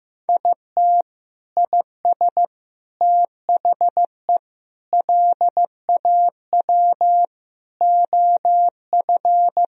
Morse code, heard as a keyed tone, about 15 wpm, 700 Hz. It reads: IT IS THE LAW OF
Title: Sentences from Top 1000 Words - Review Part 3 - 15wpm